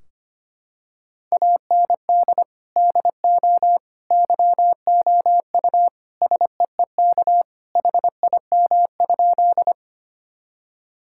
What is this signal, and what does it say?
Morse code, heard as a keyed tone, about 25 wpm, 700 Hz. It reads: AND DO YOU SEEK HIM?
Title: Sentences from Top 1600 Words 25wpm